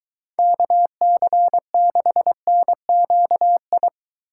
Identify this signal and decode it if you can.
Morse code, heard as a keyed tone, about 23 wpm, 700 Hz. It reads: KC6NQI